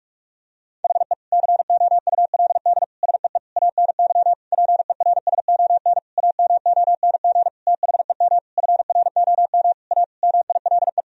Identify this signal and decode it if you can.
Morse code, heard as a keyed tone, about 46 wpm, 700 Hz. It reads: HE COULD SEE ANY PERSON AMONG THEM FROM A MILE
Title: Sentences from Top 1000 Words - Review Part 3 - 45wpm